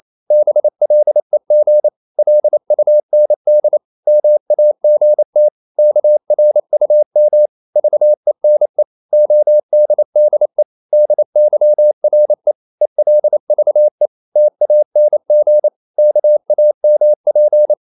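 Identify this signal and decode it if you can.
Morse code, heard as a keyed tone, about 28 wpm, 600 Hz. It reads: BLEG LUND MAGT KRUM VENE ODDE DYRE ELVE TANG KAMP
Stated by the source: Danish short words sent in Morse at 28 wpm, from CW med Gnister.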